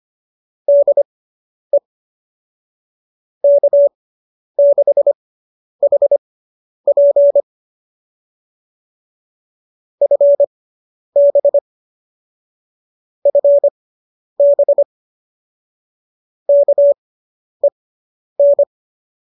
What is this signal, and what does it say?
Morse code, heard as a keyed tone, about 25 wpm, spaced as for 10 wpm, 590 Hz. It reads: DE K6HP FB FB KEN